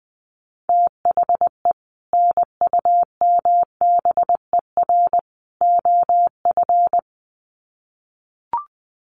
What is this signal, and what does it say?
Morse code, heard as a keyed tone, about 20 wpm, 700 Hz. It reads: THE NUMBER OF
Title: Sets of 3 Words 20wpm